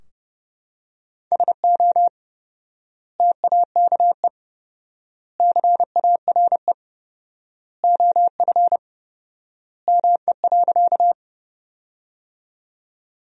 Morse code, heard as a keyed tone, about 30 wpm, 700 Hz. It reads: SO TAKE CARE OF ME.
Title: 4x Word-Spacing - Sentences from Top 300 Words - 30wpm